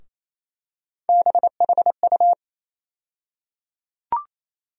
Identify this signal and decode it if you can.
Morse code, heard as a keyed tone, about 28 wpm, 700 Hz. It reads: BHU
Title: USA Call Sign Postfixes - 3 Letters 28wpm